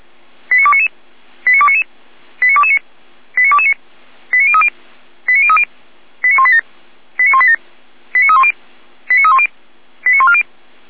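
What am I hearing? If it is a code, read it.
Five-tone selective call (ZVEI), heard as quick runs of five tones, tones of about 70 ms. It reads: 89299 89299 89229 89229 89030 89030 89187 89187 89310 89310 89140
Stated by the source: ZVEI five-tone selective call 89174 received on an SDR.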